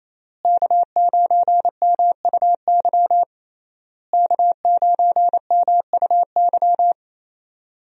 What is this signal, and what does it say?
Morse code, heard as a keyed tone, about 28 wpm, 700 Hz. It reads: K9MUY K9MUY
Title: USA Call Signs - 5 Characters - Repeated 3x - 28wpm